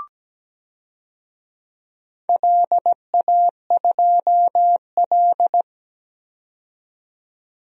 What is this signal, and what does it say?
Morse code, heard as a keyed tone, about 17 wpm, 700 Hz. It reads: LA2L
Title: DXCC-100 Call Signs - 4 Characters - 17wpm